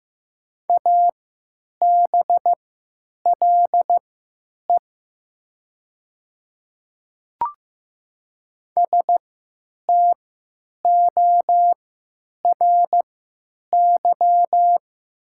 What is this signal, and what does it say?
Morse code, heard as a keyed tone, about 15 wpm, spaced as for 9 wpm, 700 Hz. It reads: ABLE STORY
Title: 3x Character Spacing - Top 500 Words - 15wpm